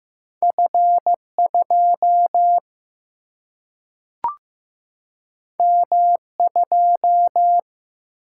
Morse code, heard as a keyed tone, about 15 wpm, 700 Hz. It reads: F2 M2
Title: Single-Letter Single-Number 15wpm